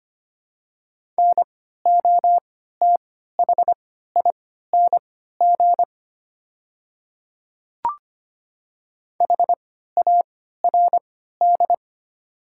Morse code, heard as a keyed tone, about 25 wpm, spaced as for 14 wpm, 700 Hz. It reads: NOTHING HARD
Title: 3x Character Spacing - Top 500 Words - 25wpm